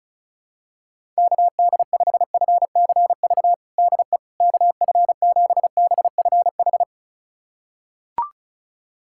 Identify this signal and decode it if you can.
Morse code, heard as a keyed tone, about 36 wpm, 700 Hz. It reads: KD5FCV DE KF7BFH
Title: USA Callsign DE Callsign 35wpm